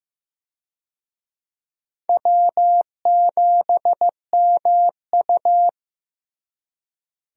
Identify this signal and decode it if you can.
Morse code, heard as a keyed tone, about 15 wpm, 700 Hz. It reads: W7MU
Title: USA Call Signs - 4 Characters - Repeated 2x - 15wpm